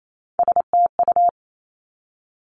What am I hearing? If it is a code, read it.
Morse code, heard as a keyed tone, about 28 wpm, 700 Hz. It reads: STU
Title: Three Challenging Letter-Number Combinations 28wpm